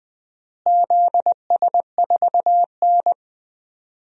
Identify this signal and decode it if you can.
Morse code, heard as a keyed tone, about 20 wpm, 700 Hz. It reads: ZS4N
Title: DXCC-100 Call Signs - 4 Characters - Repeated 3x - 20wpm